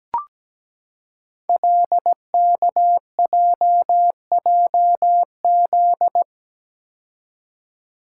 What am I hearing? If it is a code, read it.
Morse code, heard as a keyed tone, about 17 wpm, 700 Hz. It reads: LKJJZ
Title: Code Groups - 5 Letters - 17wpm